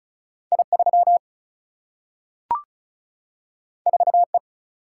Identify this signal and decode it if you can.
Morse code, heard as a keyed tone, about 36 wpm, 700 Hz. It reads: I 3 4 E
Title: Single-Letter Single-Number 35wpm